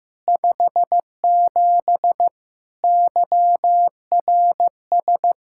Morse code, heard as a keyed tone, about 15 wpm, 700 Hz. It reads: 57 YRS
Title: Practice QSO - Exchange 3 - More Info - 15wpm